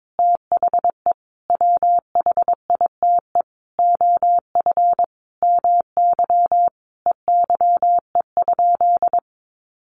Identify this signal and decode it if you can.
Morse code, heard as a keyed tone, about 22 wpm, 700 Hz. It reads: THE WHITE OF MY EYE?
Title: Sentences from Top 500 Words 22wpm